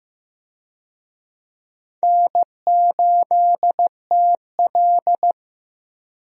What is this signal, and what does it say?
Morse code, heard as a keyed tone, about 15 wpm, 700 Hz. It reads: N8TL
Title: USA Call Signs - 4 Characters - Repeated 3x - 15wpm